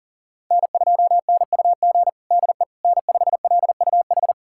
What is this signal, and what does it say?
Morse code, heard as a keyed tone, about 40 wpm, 700 Hz. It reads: N2NUG DE N5LUH
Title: USA Callsign DE Callsign - 5 Characters - 40wpm